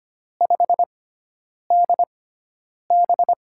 Morse code, heard as a keyed tone, about 25 wpm, spaced as for 9 wpm, 700 Hz. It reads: 5DB
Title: USA Call Signs - 4 Characters - 6x Character Spacing - 25wpm